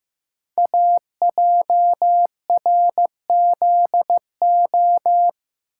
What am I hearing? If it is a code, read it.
Morse code, heard as a keyed tone, about 15 wpm, 700 Hz. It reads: AJRZO
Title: Code Groups - 5 Letter:Numbers - 15wpm